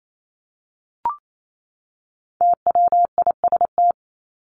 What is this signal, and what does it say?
Morse code, heard as a keyed tone, about 28 wpm, 700 Hz. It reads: TWIST
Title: Top 1751-2000 Words 28wpm